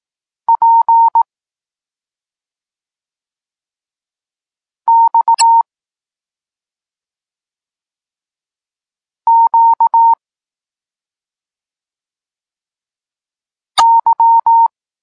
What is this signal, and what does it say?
Morse code, heard as a keyed tone, about 18 wpm, 920 Hz. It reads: P X Q Y
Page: Tekst 32 – Doskonalenie odbioru poznanych liter